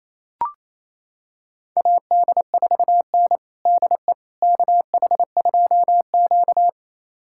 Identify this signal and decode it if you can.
Morse code, heard as a keyed tone, about 28 wpm, 700 Hz. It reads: AD4N DE KH2Q
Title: USA Callsign DE Callsign - 4 Characters - 28wpm